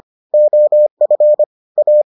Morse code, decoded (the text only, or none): OF A